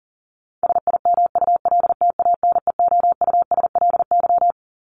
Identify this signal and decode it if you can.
Morse code, heard as a keyed tone, about 37 wpm, 700 Hz. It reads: SIMULTANEOUSLY
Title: LY Suffix - Top 100 Words with Base Words 40wpm